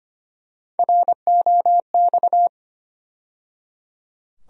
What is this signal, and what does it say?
Morse code, heard as a keyed tone, about 25 wpm, 700 Hz. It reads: ROX
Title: USA Call Sign Postfixes - 3 Letters 25wpm